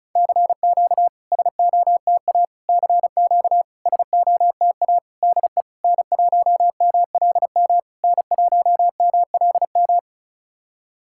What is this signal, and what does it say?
Morse code, heard as a keyed tone, about 36 wpm, 690 Hz. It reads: CQ SOTA CQ SOTA DE N1MLM N1MLM
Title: SOTA QSO Practice 35wpm